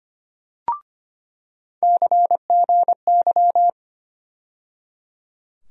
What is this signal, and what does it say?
Morse code, heard as a keyed tone, about 25 wpm, 700 Hz. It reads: CGY
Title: USA Call Sign Postfixes - 3 Letters 25wpm